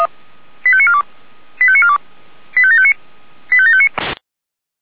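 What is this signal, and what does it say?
Five-tone selective call (ZVEI), heard as quick runs of five tones, tones of about 70 ms. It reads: 85942 85942 85759 85759
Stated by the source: ZVEI five-tone call received by SDR.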